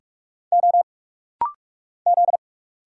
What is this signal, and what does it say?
Morse code, heard as a keyed tone, about 45 wpm, 690 Hz. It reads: O Z
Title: Single Letters - Rapid-Fire 45wpm